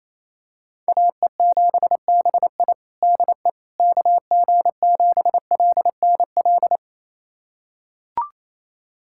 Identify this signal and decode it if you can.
Morse code, heard as a keyed tone, about 28 wpm, 700 Hz. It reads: AE7BI DE KG7LNL
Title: USA Callsign DE Callsign 28wpm